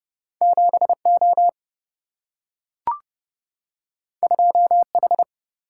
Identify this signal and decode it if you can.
Morse code, heard as a keyed tone, about 30 wpm, 700 Hz. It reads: 7 O 2 H
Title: Single-Letter Single-Number 30wpm